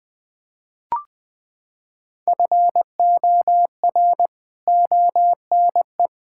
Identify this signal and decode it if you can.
Morse code, heard as a keyed tone, about 20 wpm, 700 Hz. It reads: FOR ONE
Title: Sentences from Top 1000 Words - Review Part 6 - 20wpm